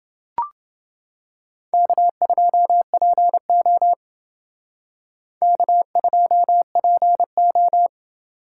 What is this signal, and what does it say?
Morse code, heard as a keyed tone, about 28 wpm, 700 Hz. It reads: K2PO K2PO